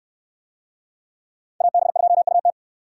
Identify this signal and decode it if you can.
Morse code, heard as a keyed tone, about 69 wpm, 690 Hz. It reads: AB2UT